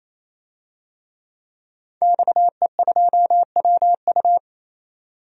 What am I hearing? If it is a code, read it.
Morse code, heard as a keyed tone, about 27 wpm, 700 Hz. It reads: XE2WU
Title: DXCC-100 Call Signs - 5 Characters - 28wpm